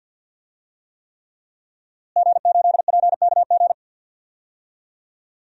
Morse code, heard as a keyed tone, about 50 wpm, 700 Hz. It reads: G8PKG
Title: DXCC-100 Call Signs - 5 Characters - Repeated 3x - 50wpm